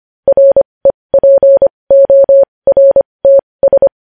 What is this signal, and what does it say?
Morse code, heard as a keyed tone, about 25 wpm, spaced as for 20 wpm, 550 Hz. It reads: REPORTS
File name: podcast / News headlines-1x-25x20